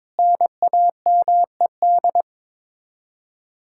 Morse code, heard as a keyed tone, about 22 wpm, 700 Hz. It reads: NAMED
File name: practice / 22wpm / Sets of 7 Words - 22wpm